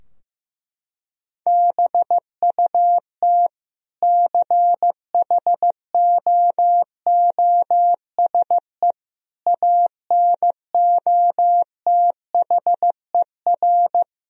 Morse code, heard as a keyed tone, about 16 wpm, 700 Hz. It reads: BUT CHOOSE ANOTHER
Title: Sentences from Top 1000 Words - Review Part 4 - 15wpm